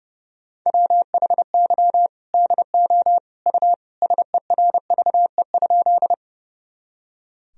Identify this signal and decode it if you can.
Morse code, heard as a keyed tone, about 30 wpm, 700 Hz. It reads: WHY DO U SERVE?
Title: Sentences from Top 400 Words and Abbreviations 30wpm